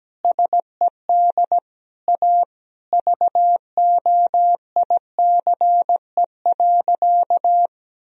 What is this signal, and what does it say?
Morse code, heard as a keyed tone, about 17 wpm, 700 Hz. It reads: SED A VOICE.
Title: Sentences from Top 500 Words and Abbreviations 17wpm